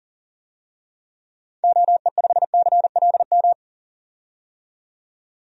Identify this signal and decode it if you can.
Morse code, heard as a keyed tone, about 40 wpm, 700 Hz. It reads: OE5CLM
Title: DXCC-100 Call Signs - 6 Characters - 40wpm